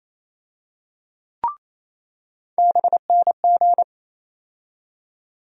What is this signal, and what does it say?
Morse code, heard as a keyed tone, about 27 wpm, 700 Hz. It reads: BNG
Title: USA Call Sign Postfixes - 3 Letters 28wpm